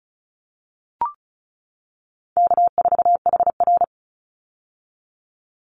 Morse code, heard as a keyed tone, about 36 wpm, 700 Hz. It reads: K4HR
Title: Call Signs - 35wpm